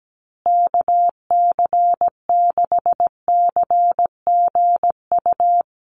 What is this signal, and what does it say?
Morse code, heard as a keyed tone, about 17 wpm, 700 Hz. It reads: KC6CGU
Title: USA Call Signs Repeated 2x 17wpm